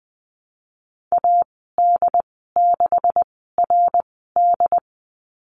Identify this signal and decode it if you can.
Morse code, heard as a keyed tone, about 20 wpm, spaced as for 15 wpm, 700 Hz. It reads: AD6RD
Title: 154 Lesson 25g - Letter p - Call Signs 20wpm